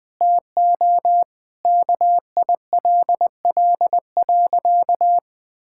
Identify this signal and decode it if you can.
Morse code, heard as a keyed tone, about 20 wpm, 700 Hz. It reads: TO KILL.
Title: Sentences from Top 1600 Words 20wpm